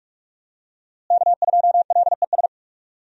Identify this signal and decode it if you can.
Morse code, heard as a keyed tone, about 46 wpm, 700 Hz. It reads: K2LES
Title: USA Call Signs - 5 Characters - Repeated 2x - 45wpm